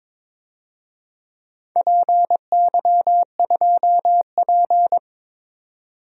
Morse code, heard as a keyed tone, about 22 wpm, 700 Hz. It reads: PY2P
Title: DXCC-100 Call Signs - 4 Characters - 22wpm